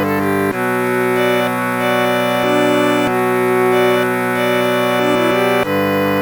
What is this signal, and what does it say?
There is no code.